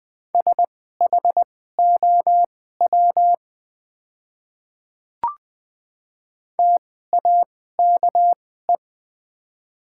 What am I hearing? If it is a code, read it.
Morse code, heard as a keyed tone, about 20 wpm, spaced as for 14 wpm, 700 Hz. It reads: SHOW TAKE
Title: 2x Character Spacing - Top 200 Words - 20wpm